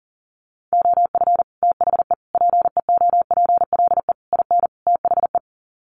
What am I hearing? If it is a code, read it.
Morse code, heard as a keyed tone, about 37 wpm, 700 Hz. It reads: OF THE PEOPLE IN THE
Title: Sets of 5 Words Encore 40wpm